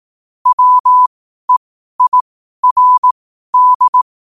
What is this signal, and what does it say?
Morse code, heard as a keyed tone, about 18 wpm, spaced as for 12 wpm, 1000 Hz. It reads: WEIRD
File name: qotdfiles / 12wpm